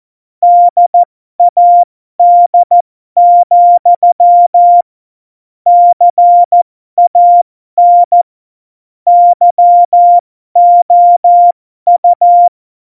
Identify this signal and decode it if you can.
Morse code, heard as a keyed tone, about 14 wpm, spaced as for 12 wpm, 700 Hz. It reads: DAD, CAN YOU
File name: Morsekurs / Witze